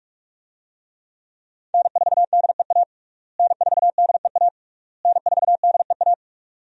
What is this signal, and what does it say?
Morse code, heard as a keyed tone, about 41 wpm, 700 Hz. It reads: N4DEA N4DEA N4DEA
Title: USA Call Signs - 5 Characters - Repeated 3x - 45wpm